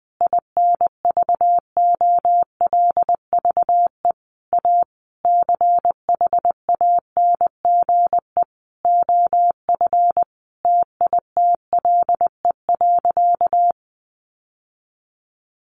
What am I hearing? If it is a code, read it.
Morse code, heard as a keyed tone, about 20 wpm, 700 Hz. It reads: INVOLVE A CHANGE OF TITLE.
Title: Sentences from Top 1200 Words 20wpm